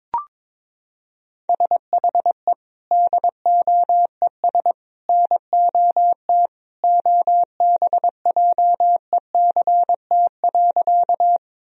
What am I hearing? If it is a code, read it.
Morse code, heard as a keyed tone, about 22 wpm, 700 Hz. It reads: SHE DOES NOT OBJECT.
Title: Sentences from Top 1000 Words - Review Part 3 - 22wpm